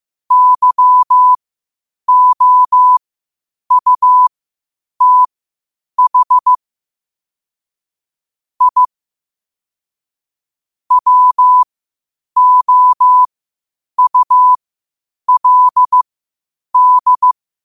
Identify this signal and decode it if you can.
Morse code, heard as a keyed tone, about 15 wpm, spaced as for 8 wpm, 1000 Hz. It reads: YOUTH I WOULD
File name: qotdfiles / 8wpm